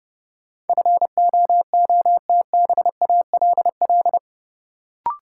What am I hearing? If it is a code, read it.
Morse code, heard as a keyed tone, about 32 wpm, 700 Hz. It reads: FOOTBALL
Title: Sentences from Top 1600 Words - New Word Drill Alt - 30wpm